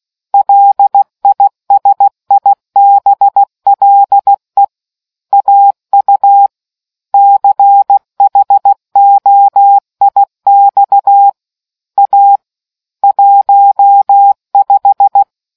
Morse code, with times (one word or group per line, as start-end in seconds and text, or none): 0.3-4.6 LISIBLE
5.3-6.5 AU
7.1-11.3 CHOIX
12.0-12.4 A
13.0-15.2 15